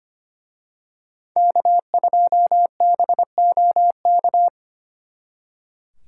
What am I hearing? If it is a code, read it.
Morse code, heard as a keyed tone, about 29 wpm, 700 Hz. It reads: K2BOK
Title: Speed Racing - USA 1x3 Call Signs 25wpm